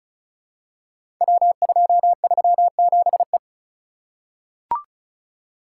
Morse code, heard as a keyed tone, about 36 wpm, 700 Hz. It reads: W237E
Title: Code Groups - 5 Letter:Numbers - 35wpm